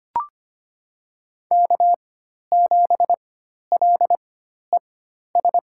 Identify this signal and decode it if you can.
Morse code, heard as a keyed tone, about 25 wpm, spaced as for 12 wpm, 700 Hz. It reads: K7LES
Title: USA Call Signs - 5 Characters - 4x Character Spacing - 25wpm